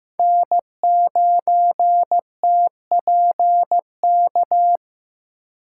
Morse code, heard as a keyed tone, about 15 wpm, 700 Hz. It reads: N9TPK
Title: USA Call Signs - 2x Character Spacing - 15wpm